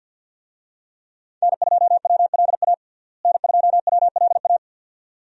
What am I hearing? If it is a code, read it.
Morse code, heard as a keyed tone, about 48 wpm, 700 Hz. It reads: N2WLA N2WLA
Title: USA Call Signs - 5 Characters - Repeated 2x - 50wpm